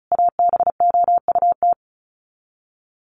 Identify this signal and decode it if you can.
Morse code, heard as a keyed tone, about 34 wpm, 700 Hz. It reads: ABOUT
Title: Sets of 5 Words Encore 35wpm